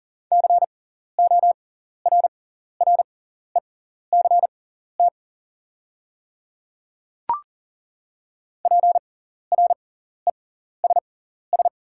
Morse code, heard as a keyed tone, about 40 wpm, spaced as for 14 wpm, 690 Hz. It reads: CORRECT PRESS
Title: 6x Character Spacing - Top 500 Words - 40wpm